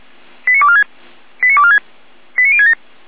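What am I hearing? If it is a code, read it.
Five-tone selective call (ZVEI), heard as quick runs of five tones, tones of about 70 ms. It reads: 89246 89246 89067
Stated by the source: ZVEI five-tone selective call received by SDR.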